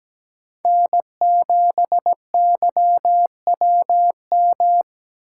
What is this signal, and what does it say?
Morse code, heard as a keyed tone, about 17 wpm, 700 Hz. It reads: N7YWM
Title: USA Call Signs - 8x Character Spacing - 17wpm